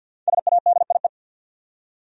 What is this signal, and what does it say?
Morse code, heard as a keyed tone, about 50 wpm, 700 Hz. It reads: SADIE
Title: USA Common Names - 50wpm